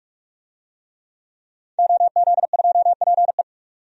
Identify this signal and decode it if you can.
Morse code, heard as a keyed tone, about 46 wpm, 700 Hz. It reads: OZ2PE
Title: DXCC-100 Call Signs - 5 Characters - 45wpm